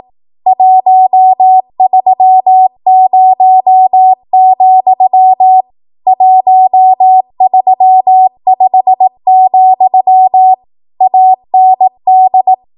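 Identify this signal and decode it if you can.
Morse code, heard as a keyed tone, about 18 wpm, 750 Hz. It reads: 130, 135, AND
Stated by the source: ARRL 18 wpm Morse code practice archive transmission.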